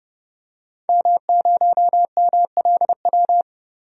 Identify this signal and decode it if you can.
Morse code, heard as a keyed tone, about 31 wpm, 700 Hz. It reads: M0MLW
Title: DXCC-100 Call Signs - 5 Characters - 30wpm